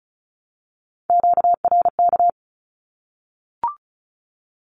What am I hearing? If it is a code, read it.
Morse code, heard as a keyed tone, about 34 wpm, 700 Hz. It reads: QRK
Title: Q-Codes Repeated 3x 35wpm